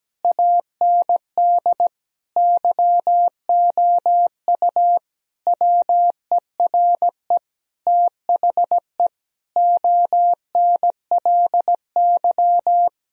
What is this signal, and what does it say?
Morse code, heard as a keyed tone, about 17 wpm, 700 Hz. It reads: AND YOU WERE THE ONLY
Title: Sentences from Top 1000 Words - Review Part 4 - 17wpm